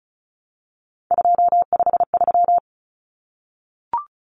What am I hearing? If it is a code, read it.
Morse code, heard as a keyed tone, about 34 wpm, 700 Hz. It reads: 253
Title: Numbers - 3-Digit 35wpm